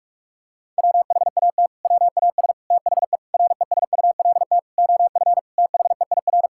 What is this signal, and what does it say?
Morse code, heard as a keyed tone, about 46 wpm, 700 Hz. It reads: WHAT WAS THE RESULT OF THEIR
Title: Sentences from Top 1000 Words - Review Part 5 - 45wpm